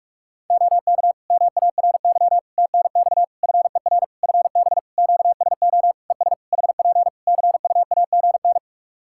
Morse code, heard as a keyed tone, about 42 wpm, 700 Hz. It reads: OK MARY TNX FER FB QSO ES HP CUAGN
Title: Practice QSO - Exchange 4 - Ending - 45wpm